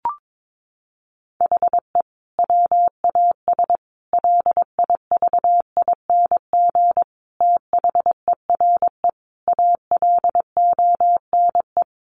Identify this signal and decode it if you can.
Morse code, heard as a keyed tone, about 22 wpm, 700 Hz. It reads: HE WAS LIVING THERE ALONE
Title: Sentences from Top 1100 Words 22wpm